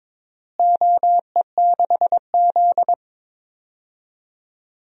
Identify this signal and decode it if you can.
Morse code, heard as a keyed tone, about 22 wpm, 700 Hz. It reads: OE6Z
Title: DXCC-100 Call Signs - 4 Characters - Repeated 3x - 22wpm